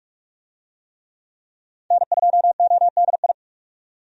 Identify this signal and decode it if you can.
Morse code, heard as a keyed tone, about 42 wpm, 700 Hz. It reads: N2ODI
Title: USA Call Signs - 5 Characters - 45wpm